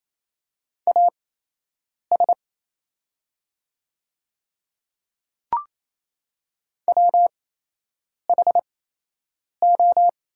Morse code, heard as a keyed tone, about 28 wpm, spaced as for 8 wpm, 690 Hz. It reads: AS WHO
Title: 8x Character Spacing - Top 100 Words - 28wpm